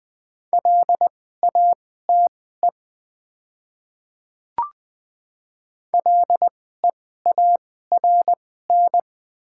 Morse code, heard as a keyed tone, about 20 wpm, spaced as for 14 wpm, 700 Hz. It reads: LATE LEARN
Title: 2x Character Spacing - Top 500 Words - 20wpm